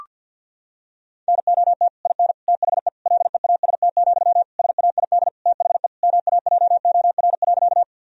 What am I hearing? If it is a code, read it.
Morse code, heard as a keyed tone, about 50 wpm, 700 Hz. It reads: NOT IN THE LEAST, SAID THE MAJOR.
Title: Sentences from Top 1000 Words - Review Part 4 - 50wpm